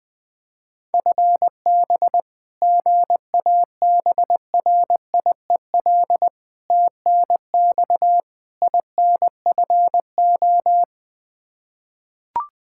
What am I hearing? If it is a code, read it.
Morse code, heard as a keyed tone, about 20 wpm, 700 Hz. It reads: FB GABRIEL TNX INFO
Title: Practice QSO - Exchange 3 - More Info - 20wpm